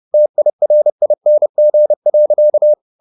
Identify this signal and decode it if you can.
Morse code, heard as a keyed tone, about 30 wpm, 600 Hz. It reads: TIRING.
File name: the old man and the sea / the old man and the sea 1 30wpm